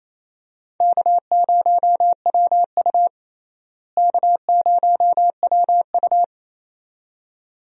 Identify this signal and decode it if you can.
Morse code, heard as a keyed tone, about 28 wpm, 700 Hz. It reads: K0WU K0WU
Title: USA Call Signs - 4 Characters - Repeated 2x - 28wpm